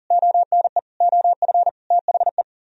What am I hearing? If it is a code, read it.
Morse code, heard as a keyed tone, about 52 wpm, 700 Hz. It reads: ONE OF THE